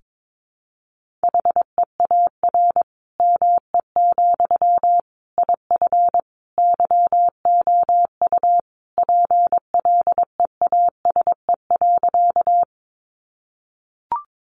Morse code, heard as a keyed tone, about 22 wpm, 700 Hz. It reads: HEAR ME, IF YOU PLEASE.
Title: Sentences from Top 1500 Words 22wpm